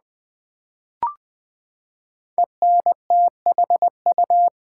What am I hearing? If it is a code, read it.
Morse code, heard as a keyed tone, about 20 wpm, 700 Hz. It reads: ENTHU
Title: Build A Word - 5 Letters At A Time - 12-Letter Words - 20wpm